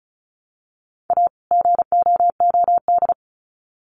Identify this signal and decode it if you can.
Morse code, heard as a keyed tone, about 34 wpm, 700 Hz. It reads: A GOOD
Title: Sets of 2 Words 35wpm